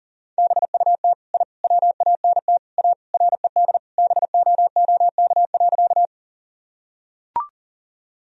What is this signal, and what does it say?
Morse code, heard as a keyed tone, about 40 wpm, 700 Hz. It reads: BUT I WANT A RED BOOK.
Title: Sentences from Top 1000 Words - Review Part 2 - 40wpm